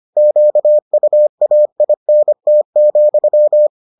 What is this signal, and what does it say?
Morse code, heard as a keyed tone, about 25 wpm, 600 Hz. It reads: QUAINT,